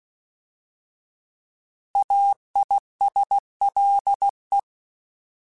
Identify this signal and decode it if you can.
Morse code, heard as a keyed tone, about 16 wpm, 800 Hz. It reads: AISLE